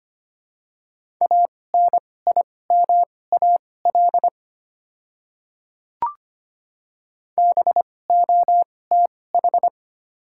Morse code, heard as a keyed tone, about 25 wpm, spaced as for 18 wpm, 690 Hz. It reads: ANIMAL BOTH
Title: 2x Character Spacing - Top 300 Words - 25wpm